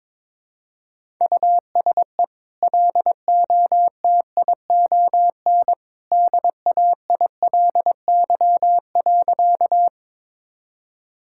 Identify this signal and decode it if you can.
Morse code, heard as a keyed tone, about 22 wpm, 700 Hz. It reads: USE LOTION DAILY.
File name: practice / 22wpm / Easy Sentences with Hints - 22wpm